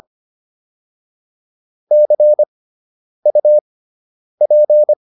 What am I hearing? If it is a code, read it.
Morse code, heard as a keyed tone, about 25 wpm, spaced as for 9 wpm, 600 Hz. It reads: CUP